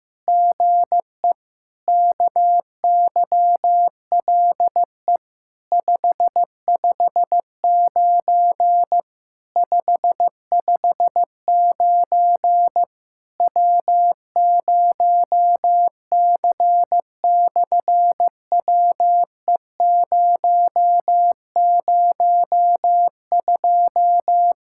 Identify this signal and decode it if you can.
Morse code, heard as a keyed tone, about 15 wpm, 700 Hz. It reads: GE KYLE 559 559 W0C/WE002
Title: SOTA QSO Practice 15wpm